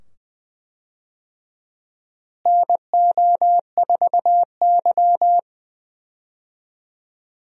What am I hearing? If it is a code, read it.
Morse code, heard as a keyed tone, about 20 wpm, 700 Hz. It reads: NO4Y
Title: USA Call Signs - 4 Characters - Repeated 2x - 20wpm